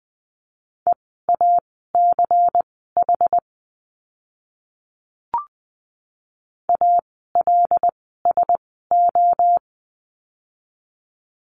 Morse code, heard as a keyed tone, about 20 wpm, spaced as for 14 wpm, 700 Hz. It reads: EACH ALSO
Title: 192 Lesson 31f - Letter v - Words 20wpm